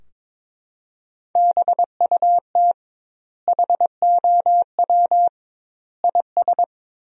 Morse code, heard as a keyed tone, about 22 wpm, 700 Hz. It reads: BUT HOW IS
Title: 2x Word-Spacing - Sentences from Top 100 Words - 22wpm